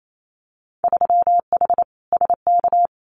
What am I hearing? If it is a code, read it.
Morse code, heard as a keyed tone, about 28 wpm, 700 Hz. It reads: 3H SK
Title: Winter Field Day Exchanges 28wpm